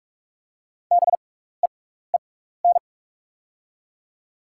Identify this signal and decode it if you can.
Morse code, heard as a keyed tone, about 45 wpm, spaced as for 16 wpm, 700 Hz. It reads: BEEN
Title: 6x Character Spacing - Top 100 Words - 45wpm